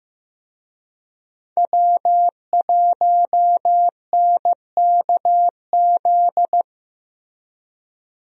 Morse code, heard as a keyed tone, about 15 wpm, 700 Hz. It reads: W1NKZ
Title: USA Call Signs - 5 Characters - Repeated 3x - 15wpm